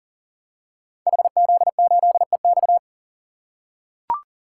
Morse code, heard as a keyed tone, about 38 wpm, 690 Hz. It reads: HZ8EX
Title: Code Groups - 5 Letter:Numbers - 40wpm